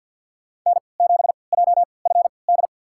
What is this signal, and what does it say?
Morse code, heard as a keyed tone, about 53 wpm, spaced as for 29 wpm, 700 Hz. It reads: N7JFD
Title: USA Call Signs - 5 Characters - 3x Character Spacing - 50wpm